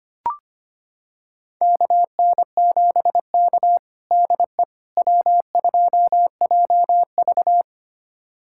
Morse code, heard as a keyed tone, about 25 wpm, 700 Hz. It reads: KN7K DE W2JV
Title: USA Callsign DE Callsign - 4 Characters - 25wpm